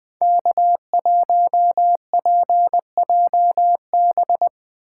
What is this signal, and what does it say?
Morse code, heard as a keyed tone, about 20 wpm, 700 Hz. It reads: K1PJB
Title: Speed Racing - USA 1x3 Call Signs 17wpm